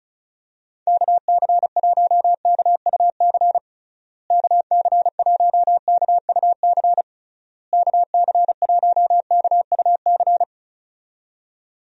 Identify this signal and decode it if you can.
Morse code, heard as a keyed tone, about 34 wpm, 700 Hz. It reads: KC1KUC KC1KUC KC1KUC
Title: USA Call Signs - 6 Characters - Repeated 3x - 35wpm